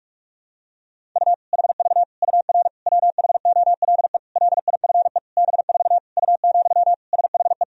Morse code, heard as a keyed tone, about 46 wpm, 700 Hz. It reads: U HV UR WHOLE LIFE B4 U, SHE
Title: Sentences from Top 400 Words and Abbreviations 45wpm